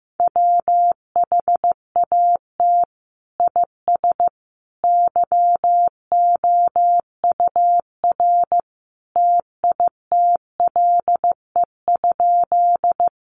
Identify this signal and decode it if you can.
Morse code, heard as a keyed tone, about 15 wpm, 700 Hz. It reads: WHAT IS YOUR TITLE?
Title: Sentences from Top 1200 Words 15wpm